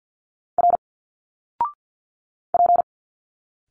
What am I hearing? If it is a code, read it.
Morse code, heard as a keyed tone, about 50 wpm, 700 Hz. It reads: R P
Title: Single Letters 50wpm